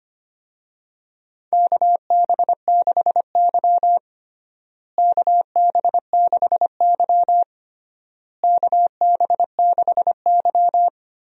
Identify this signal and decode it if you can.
Morse code, heard as a keyed tone, about 25 wpm, 700 Hz. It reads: KB6Y KB6Y KB6Y
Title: USA Call Signs - 4 Characters - Repeated 3x - 25wpm